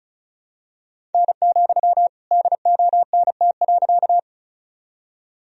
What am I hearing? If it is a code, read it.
Morse code, heard as a keyed tone, about 36 wpm, 700 Hz. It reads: N, DONT.
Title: Sentences from Top 300 Words and Abbreviations 35wpm